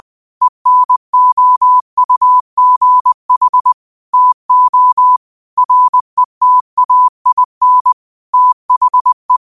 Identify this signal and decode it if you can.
Morse code, heard as a keyed tone, about 20 wpm, 1000 Hz. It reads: ENOUGH TO RETAIN THE
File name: qotdfiles / 20wpm